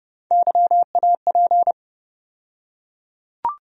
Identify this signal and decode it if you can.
Morse code, heard as a keyed tone, about 30 wpm, 700 Hz. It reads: YAP
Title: USA Call Sign Postfixes - 3 Letters 30wpm